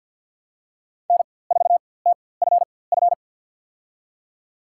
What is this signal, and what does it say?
Morse code, heard as a keyed tone, about 52 wpm, spaced as for 23 wpm, 700 Hz. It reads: N4TFF
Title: USA Call Signs - 4x Character Spacing - 50wpm